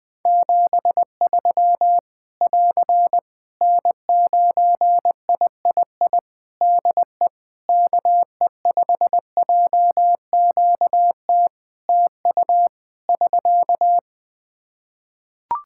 Morse code, heard as a keyed tone, about 20 wpm, 700 Hz. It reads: 73 <AR> N9III DE KE5JQT TU <SK>
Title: Practice QSO - Exchange 4 - Ending - 20wpm